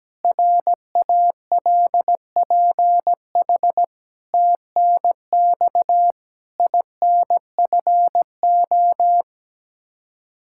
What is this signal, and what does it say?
Morse code, heard as a keyed tone, about 17 wpm, 700 Hz. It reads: RALPH TNX INFO
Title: Practice QSO - Exchange 3 - More Info - 17wpm